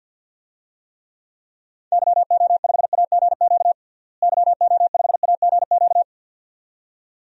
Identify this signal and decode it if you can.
Morse code, heard as a keyed tone, about 52 wpm, 700 Hz. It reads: YO5AGQ YO5AGQ
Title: DXCC-100 Call Signs - 6 Characters - Repeated 2x - 50wpm